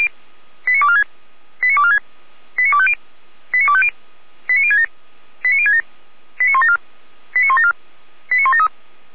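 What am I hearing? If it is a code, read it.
Five-tone selective call (ZVEI), heard as quick runs of five tones, tones of about 70 ms. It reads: 89246 89246 89250 89250 89067 89067 89174 89174 89173
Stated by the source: ZVEI five-tone call received by SDR, from the feuerwehr-orlamuende.org audio archive.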